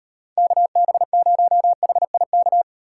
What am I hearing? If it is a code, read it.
Morse code, heard as a keyed tone, about 38 wpm, 700 Hz. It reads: KB0HIK